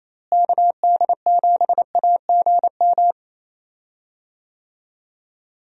Morse code, heard as a keyed tone, about 27 wpm, 700 Hz. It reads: KD7AGM
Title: USA Call Signs - 6 Characters - Repeated 2x - 28wpm